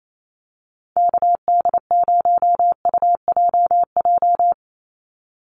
Morse code, heard as a keyed tone, about 28 wpm, 700 Hz. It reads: KD0UJJ
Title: USA Call Signs 28wpm